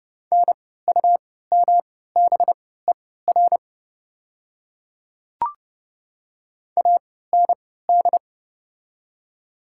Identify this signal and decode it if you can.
Morse code, heard as a keyed tone, about 29 wpm, spaced as for 17 wpm, 700 Hz. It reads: NUMBER AND
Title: 3x Character Spacing - Top 100 Words - 30wpm